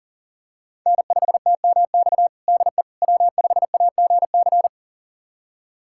Morse code, heard as a keyed tone, about 40 wpm, 700 Hz. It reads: N5TMX DE W5AGC
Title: USA Callsign DE Callsign - 5 Characters - 40wpm